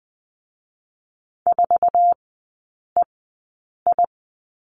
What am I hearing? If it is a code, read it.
Morse code, heard as a keyed tone, about 20 wpm, spaced as for 8 wpm, 700 Hz. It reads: 4EI
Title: 100 Lesson 17e - Letter u - Three Characters 20wpm